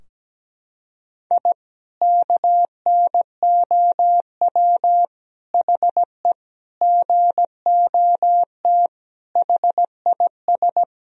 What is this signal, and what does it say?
Morse code, heard as a keyed tone, about 17 wpm, 700 Hz. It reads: I KNOW HE GOT HIS
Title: Sentences from Top 1000 Words - Review Part 3 - 17wpm